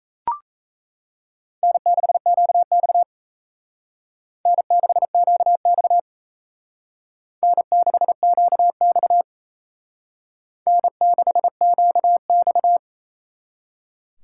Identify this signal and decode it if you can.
Morse code, heard as a keyed tone, about 35 wpm, 700 Hz. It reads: N6QX N6QX N6QX N6QX